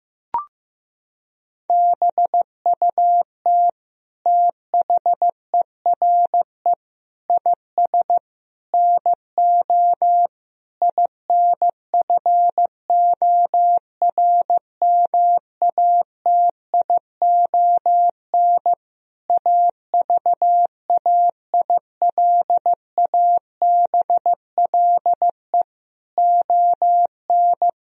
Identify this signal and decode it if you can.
Morse code, heard as a keyed tone, about 16 wpm, 700 Hz. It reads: BUT THERE IS NO INFORMATION AVAILABLE ON
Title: Sentences from Top 1000 Words - Review Part 3 - 15wpm